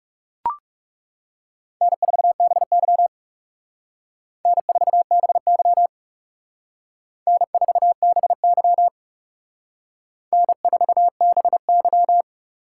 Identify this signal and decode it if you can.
Morse code, heard as a keyed tone, about 38 wpm, 700 Hz. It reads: N4BY N4BY N4BY N4BY
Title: Speed Racing - USA 1x2 Call Signs 30wpm